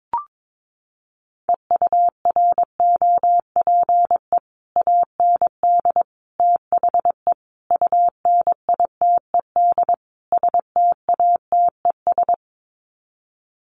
Morse code, heard as a keyed tone, about 22 wpm, 700 Hz. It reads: EUROPE AND THE UNITED STATES
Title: Sets of 5 Words Encore2 22wpm